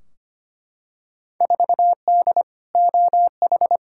Morse code, heard as a keyed tone, about 25 wpm, 700 Hz. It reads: 4D OH
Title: Field Day Exchanges Encore - 25wpm